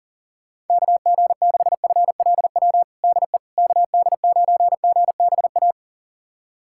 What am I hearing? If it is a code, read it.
Morse code, heard as a keyed tone, about 40 wpm, 700 Hz. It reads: KG6FLW DE KD9GBA
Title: USA Callsign DE Callsign 40wpm